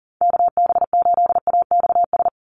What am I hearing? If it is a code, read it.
Morse code, heard as a keyed tone, about 40 wpm, 700 Hz. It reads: KB8AXS
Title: USA Call Signs 40wpm